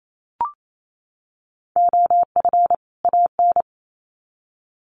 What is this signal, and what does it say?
Morse code, heard as a keyed tone, about 28 wpm, 700 Hz. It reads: OF AN
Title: Sets of 2 Words 28wpm